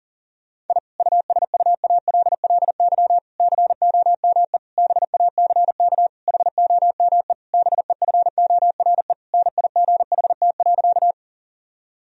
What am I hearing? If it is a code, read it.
Morse code, heard as a keyed tone, about 40 wpm, 700 Hz. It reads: I USUALLY COME BACK HOME BEFORE NIGHT.
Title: Sentences from Top 1000 Words - Review Part 3 - 40wpm